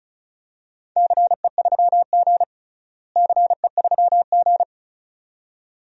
Morse code, heard as a keyed tone, about 36 wpm, 700 Hz. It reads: CE3G CE3G
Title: DXCC-100 Call Signs - 4 Characters - Repeated 2x - 35wpm